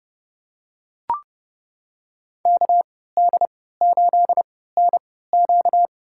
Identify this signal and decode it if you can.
Morse code, heard as a keyed tone, about 30 wpm, spaced as for 17 wpm, 700 Hz. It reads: KD8NQ
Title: USA Call Signs - 3x Character Spacing - 30wpm